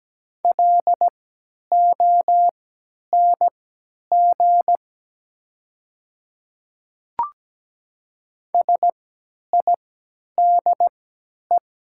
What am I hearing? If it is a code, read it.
Morse code, heard as a keyed tone, about 17 wpm, spaced as for 10 wpm, 690 Hz. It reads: LONG SIDE
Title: 3x Character Spacing - Top 100 Words - 17wpm